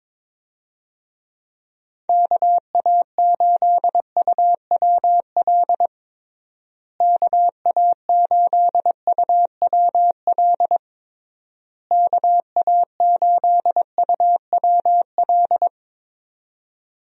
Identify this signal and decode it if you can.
Morse code, heard as a keyed tone, about 22 wpm, 700 Hz. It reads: KA8UWL KA8UWL KA8UWL
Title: USA Call Signs - 6 Characters - Repeated 3x - 22wpm